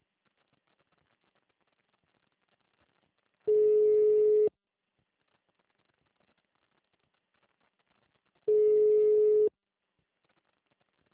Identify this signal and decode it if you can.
Ringback tone (Europe, 425 Hz).
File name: out- -2005- - - .31910